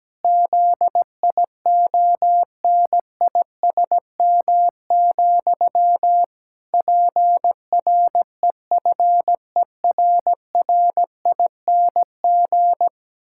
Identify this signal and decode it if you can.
Morse code, heard as a keyed tone, about 17 wpm, 700 Hz. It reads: ZIONISM, PREFERRING
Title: Pangram Challenge v2 - 17wpm